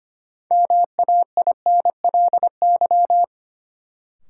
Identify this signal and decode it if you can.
Morse code, heard as a keyed tone, about 25 wpm, 700 Hz. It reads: MAINLY
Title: Sentences from Top 1400 Words - New Word Drill Alt - 25wpm